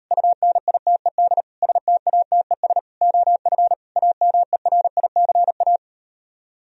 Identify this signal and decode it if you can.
Morse code, heard as a keyed tone, about 38 wpm, 700 Hz. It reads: UNITED STATES OF AMERICA